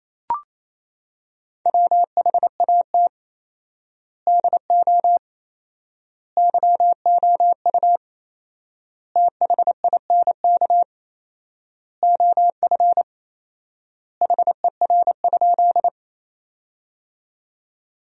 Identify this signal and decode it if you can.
Morse code, heard as a keyed tone, about 28 wpm, 700 Hz. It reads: WHAT DO YOU THINK OF HER?
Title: 4x Word-Spacing - Sentences from Top 200 Words - 28wpm